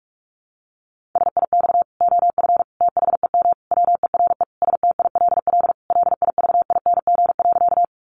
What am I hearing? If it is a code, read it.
Morse code, heard as a keyed tone, about 44 wpm, 700 Hz. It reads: SIX OF THEM WERE STILL LIVING.
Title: Sentences from Top 1100 Words 45wpm